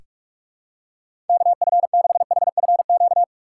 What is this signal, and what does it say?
Morse code, heard as a keyed tone, about 46 wpm, 700 Hz. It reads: KF6HFQ
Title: USA Call Signs - 3x Character Spacing - 45wpm